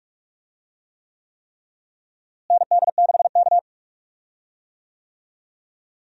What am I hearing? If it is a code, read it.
Morse code, heard as a keyed tone, about 41 wpm, 700 Hz. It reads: ND6K
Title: USA Call Signs - 4 Characters - Repeated 3x - 45wpm